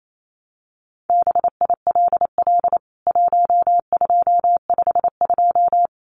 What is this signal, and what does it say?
Morse code, heard as a keyed tone, about 28 wpm, 700 Hz. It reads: BILL 1252
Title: CWT Exchange 28wpm